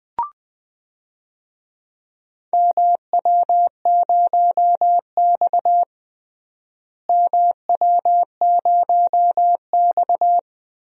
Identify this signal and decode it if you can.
Morse code, heard as a keyed tone, about 21 wpm, 700 Hz. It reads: MW0X MW0X
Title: DXCC-100 Call Signs - 4 Characters - Repeated 2x - 20wpm